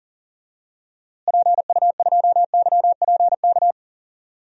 Morse code, heard as a keyed tone, about 37 wpm, 700 Hz. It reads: PU2YPK
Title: DXCC-100 Call Signs - 6 Characters - Repeated 2x - 40wpm